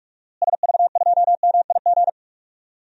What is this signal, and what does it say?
Morse code, heard as a keyed tone, about 46 wpm, 700 Hz. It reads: SV2MIG
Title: DXCC-100 Call Signs - 6 Characters - Repeated 3x - 45wpm